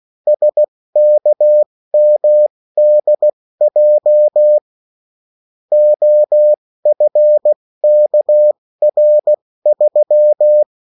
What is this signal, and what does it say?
Morse code, heard as a keyed tone, about 16 wpm, spaced as for 14 wpm, 600 Hz. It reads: SKMDJ OFKR3
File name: morse mp3 / revision9-vitesse 14 mots